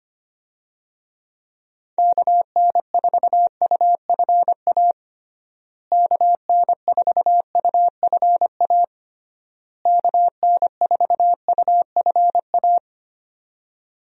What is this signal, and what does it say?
Morse code, heard as a keyed tone, about 25 wpm, 700 Hz. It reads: KN4UFA KN4UFA KN4UFA
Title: USA Call Signs - 6 Characters - Repeated 3x - 25wpm